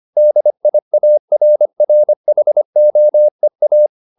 Morse code, heard as a keyed tone, about 25 wpm, 600 Hz. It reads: DIARRHOEA
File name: the old man and the sea part 1 25wpm